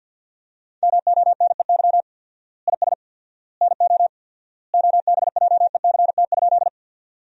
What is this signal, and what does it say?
Morse code, heard as a keyed tone, about 50 wpm, 700 Hz. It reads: MONEY IS NO OBJECT?
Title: 4x Word-Spacing - Sentences from Top 500 Words - 50wpm